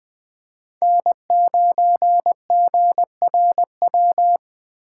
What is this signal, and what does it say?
Morse code, heard as a keyed tone, about 20 wpm, 700 Hz. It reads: N9GRW